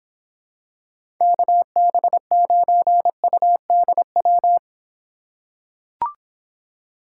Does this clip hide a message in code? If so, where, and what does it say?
Yes: Morse code from 1.2 to 4.6 s, heard as a keyed tone, about 25 wpm, 700 Hz. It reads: KB9UDW